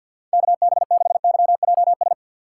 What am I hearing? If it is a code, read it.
Morse code, heard as a keyed tone, about 49 wpm, 700 Hz. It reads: KB6YJS
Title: USA Call Signs - 2x Character Spacing - 50wpm